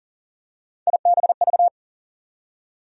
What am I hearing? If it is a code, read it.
Morse code, heard as a keyed tone, about 40 wpm, 700 Hz. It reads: IBV
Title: USA Call Sign Postfixes - 3 Letters 40wpm